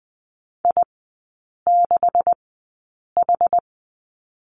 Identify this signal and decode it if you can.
Morse code, heard as a keyed tone, about 20 wpm, spaced as for 8 wpm, 700 Hz. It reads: I6H
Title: 127 Lesson 21e - Number 6 - Three Characters 20wpm